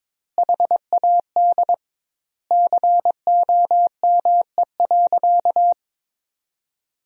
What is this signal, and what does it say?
Morse code, heard as a keyed tone, about 22 wpm, 700 Hz. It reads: HAD COME.
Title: 2x Word-Spacing - Sentences from Top 100 Words - 22wpm